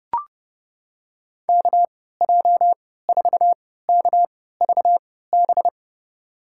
Morse code, heard as a keyed tone, about 30 wpm, spaced as for 17 wpm, 700 Hz. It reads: KJ4KVB
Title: USA Call Signs - 6 Characters - 3x Character Spacing - 30wpm